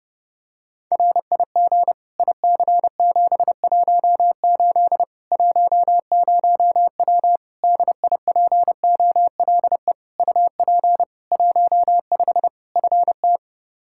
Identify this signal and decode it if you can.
Morse code, heard as a keyed tone, about 30 wpm, 700 Hz. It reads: RIG IC718 10W DIPOLE UP 15 FT
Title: Practice QSO - Exchange 2 - Info - 30wpm